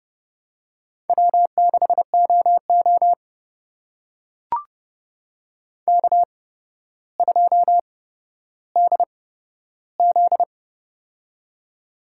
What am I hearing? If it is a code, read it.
Morse code, heard as a keyed tone, about 30 wpm, spaced as for 13 wpm, 700 Hz. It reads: W6OO K2DZ
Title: USA Call Signs - 4 Characters - 8x Character Spacing - 30wpm